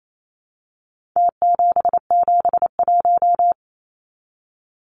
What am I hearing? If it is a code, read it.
Morse code, heard as a keyed tone, about 28 wpm, 700 Hz. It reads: T771
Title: Numbers - Contest Style - 4-Digit 28wpm